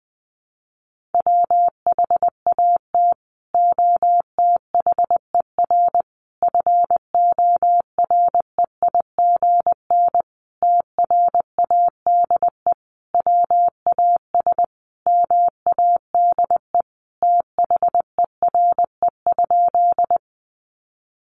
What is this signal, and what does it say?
Morse code, heard as a keyed tone, about 20 wpm, 700 Hz. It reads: WHAT OTHER FOREIGN TRADE WAS MADE THERE?
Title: Sentences from Top 1400 Words 20wpm